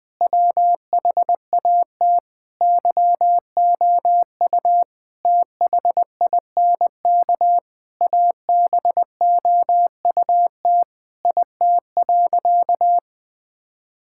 Morse code, heard as a keyed tone, about 21 wpm, 700 Hz. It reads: WHAT YOU THINK ABOUT IT.
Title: Sentences with 7-Word Phrases - Spoken Beginning - 20wpm